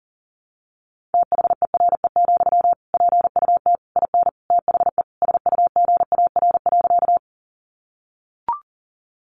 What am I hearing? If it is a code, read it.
Morse code, heard as a keyed tone, about 40 wpm, 700 Hz. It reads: THERE, PUT IN THE SUGAR.
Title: Sentences from Top 1200 Words 40wpm